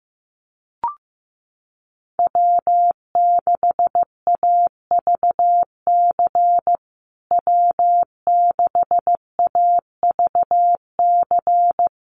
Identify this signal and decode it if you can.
Morse code, heard as a keyed tone, about 15 wpm, 700 Hz. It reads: W6AVC W6AVC
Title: USA Call Signs Repeated 3x 15wpm